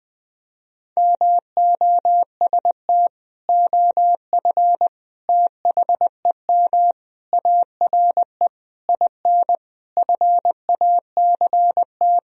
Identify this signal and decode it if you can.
Morse code, heard as a keyed tone, about 20 wpm, 700 Hz. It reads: MOST OF THEM ARE IN FACT
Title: Sentences from Top 1000 Words - Review Part 4 - 20wpm